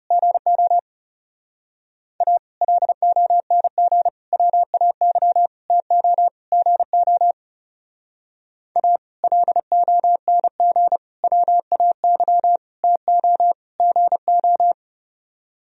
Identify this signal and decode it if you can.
Morse code, heard as a keyed tone, about 37 wpm, 700 Hz. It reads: GO A LONG WAY TO GO A LONG WAY TO GO
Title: Speed-Racing - Sets of 5 Words 30wpm